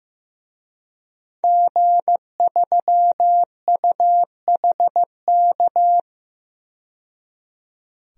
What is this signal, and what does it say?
Morse code, heard as a keyed tone, about 15 wpm, 700 Hz. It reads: G3UHK
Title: DXCC-100 Call Signs - 5 Characters - 15wpm